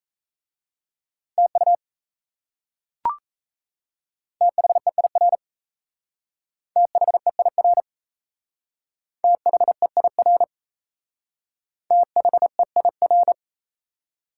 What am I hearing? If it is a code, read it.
Morse code, heard as a keyed tone, about 35 wpm, 700 Hz. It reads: TU THEIR THEIR THEIR THEIR
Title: Speed-Racing Top 180 QSO Elements 28wpm